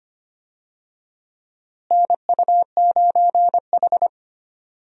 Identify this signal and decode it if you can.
Morse code, heard as a keyed tone, about 25 wpm, 700 Hz. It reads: NU9H
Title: USA Call Signs - 4 Characters - Repeated 3x - 25wpm